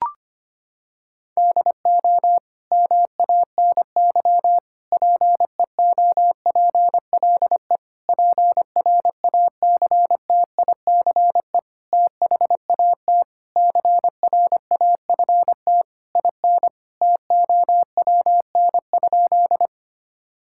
Morse code, heard as a keyed tone, about 25 wpm, 700 Hz. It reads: DO MANY PEOPLE PRACTICE THAT CRAFT IN TOWN?
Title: Sentences from Top 1000 Words - Review Part 6 - 25wpm